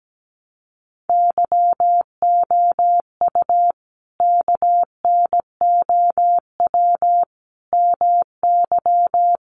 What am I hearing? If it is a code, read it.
Morse code, heard as a keyed tone, about 17 wpm, 700 Hz. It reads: YOU KNOW MY
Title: Sentences from Top 600 Words 17wpm